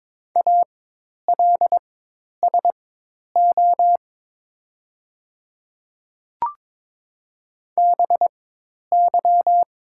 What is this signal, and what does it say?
Morse code, heard as a keyed tone, about 22 wpm, spaced as for 10 wpm, 700 Hz. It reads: ALSO BY
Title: 4x Character Spacing - Top 300 Words - 22wpm